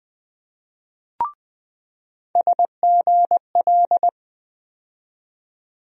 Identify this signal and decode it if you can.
Morse code, heard as a keyed tone, about 20 wpm, 700 Hz. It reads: SGL
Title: USA Call Sign Postfixes - 3 Letters 20wpm